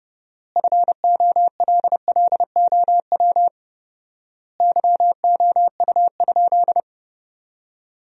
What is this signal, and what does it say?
Morse code, heard as a keyed tone, about 31 wpm, 700 Hz. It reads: FOLLOW YOU?
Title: 4x Word-Spacing - Sentences from Top 200 Words - 30wpm